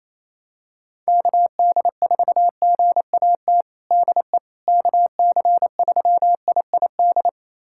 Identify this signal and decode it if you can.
Morse code, heard as a keyed tone, about 28 wpm, 700 Hz. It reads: KD4GAT DE KC3IID
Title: USA Callsign DE Callsign - 6 Characters - 28wpm